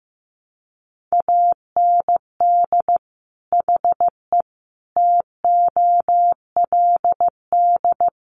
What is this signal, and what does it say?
Morse code, heard as a keyed tone, about 14 wpm, 700 Hz. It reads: AND HE TOLD
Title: Sentences from Top 500 Words 15wpm